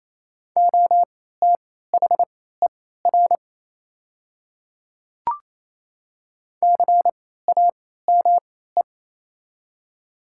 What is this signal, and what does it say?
Morse code, heard as a keyed tone, about 27 wpm, spaced as for 16 wpm, 690 Hz. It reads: OTHER CAME
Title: 3x Character Spacing - Top 200 Words - 28wpm